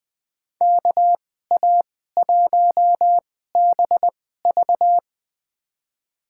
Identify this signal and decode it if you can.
Morse code, heard as a keyed tone, about 20 wpm, spaced as for 15 wpm, 700 Hz. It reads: KA1BV
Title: USA Call Signs - 5 Characters - 2x Character Spacing - 20wpm